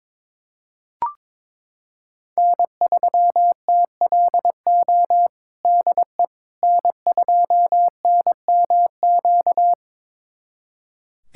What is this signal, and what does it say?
Morse code, heard as a keyed tone, about 22 wpm, 700 Hz. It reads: N3TLO DE N2NMQ
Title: USA Callsign DE Callsign - 5 Characters - 22wpm